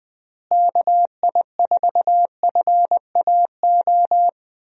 Morse code, heard as a keyed tone, about 20 wpm, 700 Hz. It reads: KI4FAO
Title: USA Call Signs - 6 Characters - Repeated 3x - 20wpm